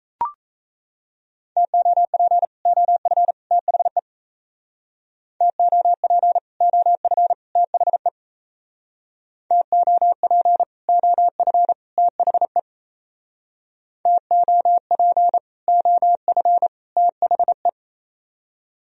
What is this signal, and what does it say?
Morse code, heard as a keyed tone, about 35 wpm, 700 Hz. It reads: TOP OF THE TOP OF THE TOP OF THE TOP OF THE